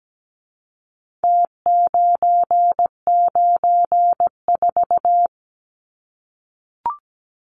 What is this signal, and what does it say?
Morse code, heard as a keyed tone, about 17 wpm, 700 Hz. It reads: T994
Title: Numbers - Contest Style - 4-Digit 17wpm